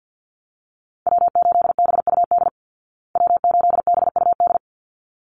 Morse code, heard as a keyed tone, about 48 wpm, 700 Hz. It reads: W8BUD W8BUD
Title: Call Signs Repeated 3x - 50wpm